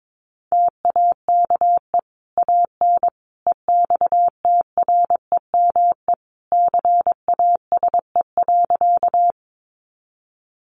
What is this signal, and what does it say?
Morse code, heard as a keyed tone, about 22 wpm, 700 Hz. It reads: TAKE AN EXTREME CASE.
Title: Sentences from Top 1300 Words 22wpm